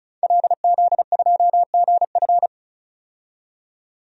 Morse code, heard as a keyed tone, about 36 wpm, 700 Hz. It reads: LZ2GF
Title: DXCC-100 Call Signs - 5 Characters - 35wpm